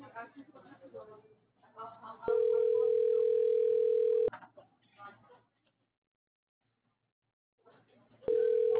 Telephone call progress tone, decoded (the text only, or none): ringback tone